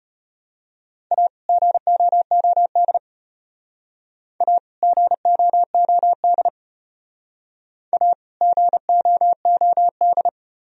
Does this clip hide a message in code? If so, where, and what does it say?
Yes: Morse code from 1.1 to 10.3 s, heard as a keyed tone, about 32 wpm, 700 Hz. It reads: A GOOD A GOOD A GOOD